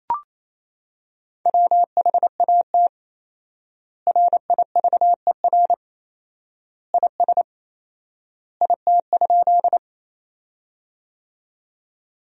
Morse code, heard as a keyed tone, about 28 wpm, 700 Hz. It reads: WHAT RIVER IS IT?
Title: 4x Word-Spacing - Sentences from Top 300 Words - 28wpm